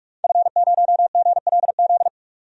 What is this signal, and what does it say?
Morse code, heard as a keyed tone, about 46 wpm, 700 Hz. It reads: F0GLZ
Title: DXCC-100 Call Signs - 5 Characters - 45wpm